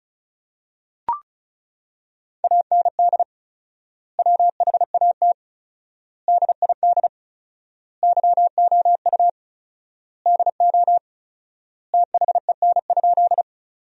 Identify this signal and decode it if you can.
Morse code, heard as a keyed tone, about 34 wpm, 700 Hz. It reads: AND WHAT DID YOU DO THEN?
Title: 4x Word-Spacing - Sentences from Top 100 Words - 35wpm